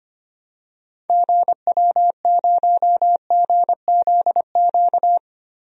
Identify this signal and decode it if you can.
Morse code, heard as a keyed tone, about 25 wpm, 700 Hz. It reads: GW0GZQ
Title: DXCC-100 Call Signs - 6 Characters - 25wpm